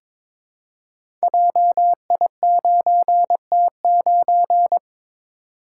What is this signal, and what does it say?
Morse code, heard as a keyed tone, about 23 wpm, 700 Hz. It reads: JI9T9
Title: Code Groups - 5 Letter:Numbers - 22wpm